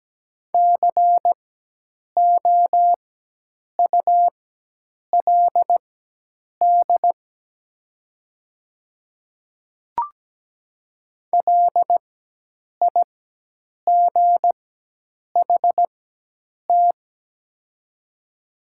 Morse code, heard as a keyed tone, about 17 wpm, spaced as for 8 wpm, 700 Hz. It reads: COULD LIGHT
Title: 4x Character Spacing - Top 200 Words - 17wpm